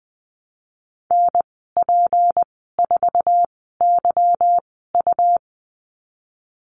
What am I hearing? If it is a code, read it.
Morse code, heard as a keyed tone, about 20 wpm, spaced as for 14 wpm, 700 Hz. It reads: NP4YU
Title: 179 Lesson 29g - Slash - Call Signs 20wpm